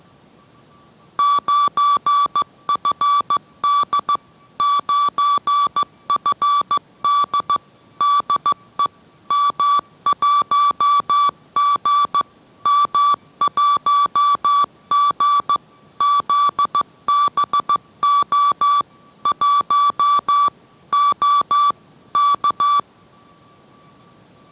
Morse code, heard as a keyed tone, about 15 wpm, 1150 Hz. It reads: 9FD 9FD DE M1G M1G ZBO 1 O K